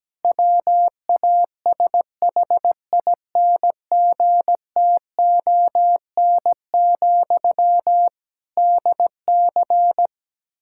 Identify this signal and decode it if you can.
Morse code, heard as a keyed tone, about 17 wpm, 700 Hz. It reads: WASHINGTON, DC
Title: K1USN SST QSO Practice 17wpm